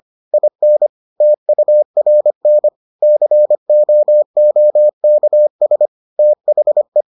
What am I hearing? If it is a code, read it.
Morse code, heard as a keyed tone, about 25 wpm, 600 Hz. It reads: IN TURN COOKS THE